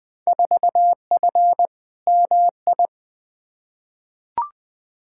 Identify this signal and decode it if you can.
Morse code, heard as a keyed tone, about 20 wpm, 700 Hz. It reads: 4F MI
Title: Field Day Exchanges Encore - 20wpm